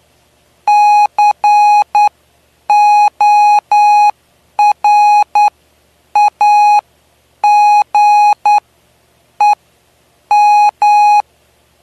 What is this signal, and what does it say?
Morse code, heard as a keyed tone, about 9 wpm, spaced as for 7 wpm, 830 Hz. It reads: CORAGEM
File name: cw / py2lcd / 09-PY2LCD-Li��o 9